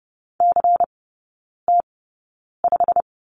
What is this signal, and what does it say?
Morse code, heard as a keyed tone, about 30 wpm, spaced as for 9 wpm, 700 Hz. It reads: CT5
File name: Lesson 16e - Letter c - Three Characters 30wpm